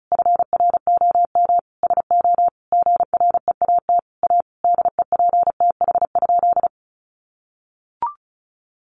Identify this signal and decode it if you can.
Morse code, heard as a keyed tone, about 34 wpm, 700 Hz. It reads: FROM SO GREAT A DEPTH?
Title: Sentences from Top 1200 Words 35wpm